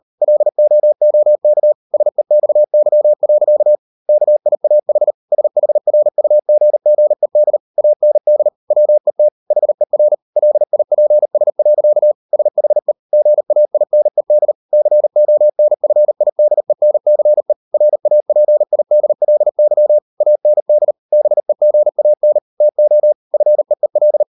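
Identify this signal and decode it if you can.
Morse code, heard as a keyed tone, about 38 wpm, 600 Hz. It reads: LOOK SEXY. KIAH SHRUGGED AND WET HER LIPS. SHE GAINED CONFIDENCE RAPIDLY AND BEGAN TO FEEL
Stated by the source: Morse code practice at 38 WPM, sending Ken Follett's novel Never in English, from CW med Gnister.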